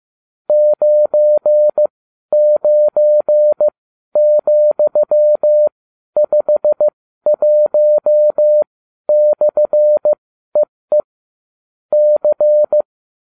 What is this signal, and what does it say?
Morse code, heard as a keyed tone, about 15 wpm, spaced as for 12 wpm, 600 Hz. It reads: 99,51/EE C